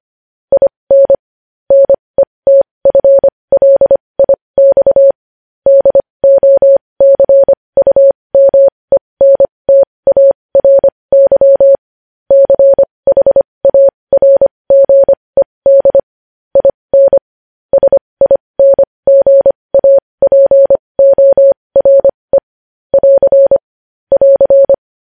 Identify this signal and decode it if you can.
Morse code, heard as a keyed tone, about 25 wpm, spaced as for 20 wpm, 550 Hz. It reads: IN NETFLIX DOCUMENTARY CHARGED IN SINGAPORE <AR> <AR>
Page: news headlines in morse code at 20 wpm